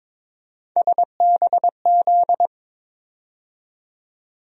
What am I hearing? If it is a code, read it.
Morse code, heard as a keyed tone, about 21 wpm, 700 Hz. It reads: SBZ
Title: USA Call Sign Postfixes - 3 Letters 22wpm